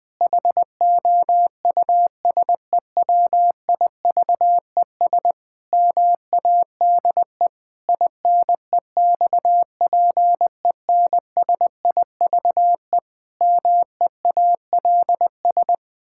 Morse code, heard as a keyed tone, about 21 wpm, 700 Hz. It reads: HOUSEWIVES MADE INEXPENSIVE MEALS
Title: Pangram Challenge v2 - 20wpm